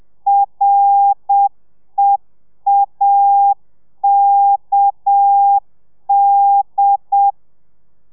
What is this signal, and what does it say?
Morse code, heard as a keyed tone, about 7 wpm, 800 Hz. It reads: REAKD